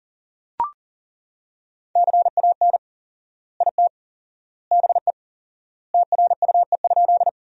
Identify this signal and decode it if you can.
Morse code, heard as a keyed tone, about 40 wpm, 700 Hz. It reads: CAN IT BE TRUE?
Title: 4x Word-Spacing - Sentences from Top 400 Words - 40wpm